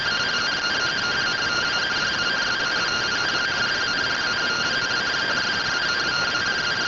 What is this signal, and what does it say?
Two-tone teleprinter signal, about 170 Hz shift.